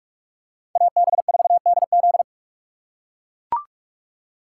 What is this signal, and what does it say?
Morse code, heard as a keyed tone, about 43 wpm, 700 Hz. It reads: AB4DZ
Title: USA Call Signs - 5 Characters - 2x Character Spacing - 45wpm